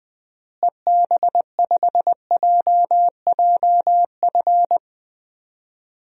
Morse code, heard as a keyed tone, about 20 wpm, 700 Hz. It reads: EB5JJF
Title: DXCC-100 Call Signs - 6 Characters - Repeated 2x - 20wpm